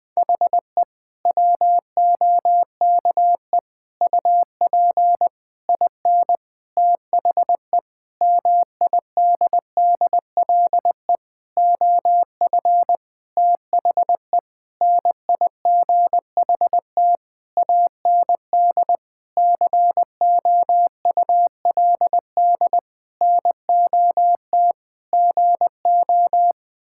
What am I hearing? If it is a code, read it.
Morse code, heard as a keyed tone, about 21 wpm, 700 Hz. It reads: HE WOKE UP IN THE MIDDLE OF THE NIGHT AND COULD NOT GO
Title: Sentences with 7-Word Phrases - 20wpm